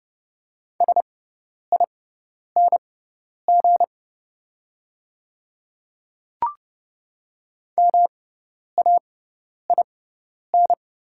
Morse code, heard as a keyed tone, about 30 wpm, spaced as for 10 wpm, 690 Hz. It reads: SING MAIN
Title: 6x Character Spacing - Top 500 Words - 30wpm